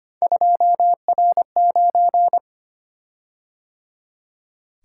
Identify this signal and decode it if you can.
Morse code, heard as a keyed tone, about 25 wpm, 700 Hz. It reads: 2R9
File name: Three Letter-Number Combinations - 25wpm